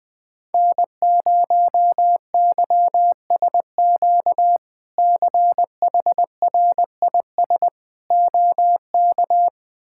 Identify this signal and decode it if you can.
Morse code, heard as a keyed tone, about 20 wpm, 700 Hz. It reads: N0YSQ CHRIS OK
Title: K1USN SST QSO Practice - 20wpm